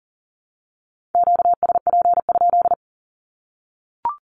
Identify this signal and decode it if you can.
Morse code, heard as a keyed tone, about 39 wpm, 700 Hz. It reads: QSP?
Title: Q-Codes Repeated 3x 40wpm